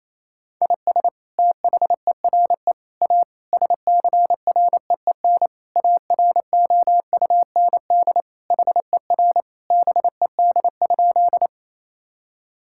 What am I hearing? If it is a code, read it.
Morse code, heard as a keyed tone, about 28 wpm, 700 Hz. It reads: IS THERE A SCREEN AROUND HER BED?
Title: Sentences from Top 1000 Words - Review Part 6 - 28wpm